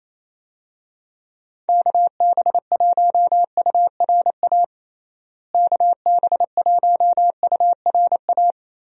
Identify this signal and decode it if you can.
Morse code, heard as a keyed tone, about 28 wpm, 700 Hz. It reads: KB1URA KB1URA
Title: USA Call Signs - 6 Characters - Repeated 3x - 28wpm